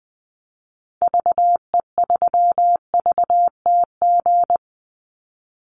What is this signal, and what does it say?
Morse code, heard as a keyed tone, about 20 wpm, 700 Hz. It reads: VE3VTG
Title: Call Signs - 20wpm